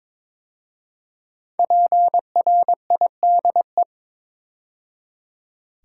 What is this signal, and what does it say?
Morse code, heard as a keyed tone, about 22 wpm, 700 Hz. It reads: PRIDE
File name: 5 Letter Words - First 3 Letters Given - 22wpm